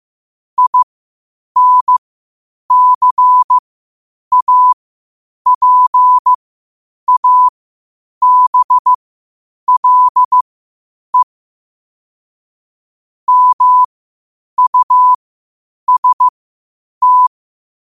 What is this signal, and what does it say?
Morse code, heard as a keyed tone, about 15 wpm, spaced as for 8 wpm, 1000 Hz. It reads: INCAPABLE MUST
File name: qotdfiles / 8wpm